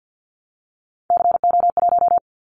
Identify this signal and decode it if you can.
Morse code, heard as a keyed tone, about 50 wpm, 700 Hz. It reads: CO1
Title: Callsign Prefixes for Top 100 DXCC - 50wpm